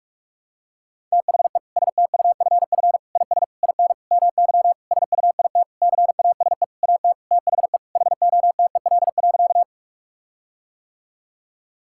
Morse code, heard as a keyed tone, about 41 wpm, 700 Hz. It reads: THE STUFF IS IN MY SUIT CASE AT THE HOTEL.
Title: Sentences from Top 1600 Words 45wpm